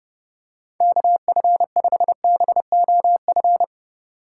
Morse code, heard as a keyed tone, about 30 wpm, 700 Hz. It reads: KF5BOF